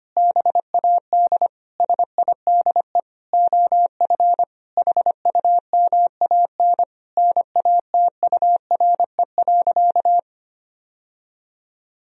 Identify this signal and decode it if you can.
Morse code, heard as a keyed tone, about 25 wpm, 700 Hz. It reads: BAD SIDE OF HUMAN NATURE.
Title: Sentences from Top 1000 Words - Review Part 3 - 25wpm